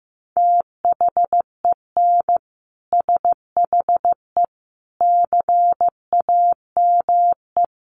Morse code, heard as a keyed tone, about 15 wpm, 700 Hz. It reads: THEN SHE CAME
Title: Sentences from Top 1300 Words 15wpm